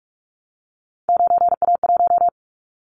Morse code, heard as a keyed tone, about 46 wpm, 700 Hz. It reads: 9A1
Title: Callsign Prefixes for Top 25 DXCC - 45wpm